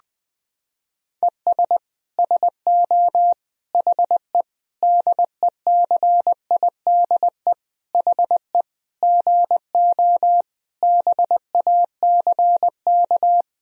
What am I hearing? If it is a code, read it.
Morse code, heard as a keyed tone, about 20 wpm, 700 Hz. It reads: ES SO HE DECIDE HE GO BACK
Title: Sentences from Top 500 Words and Abbreviations 20wpm